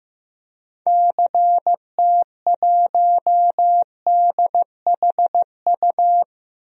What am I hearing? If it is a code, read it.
Morse code, heard as a keyed tone, about 15 wpm, 700 Hz. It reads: CT1DHU
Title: DXCC-100 Call Signs - 6 Characters - Repeated 3x - 15wpm